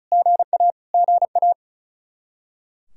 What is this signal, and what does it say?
Morse code, heard as a keyed tone, about 34 wpm, 700 Hz. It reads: GA GA
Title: K1USN SST QSO Practice 35wpm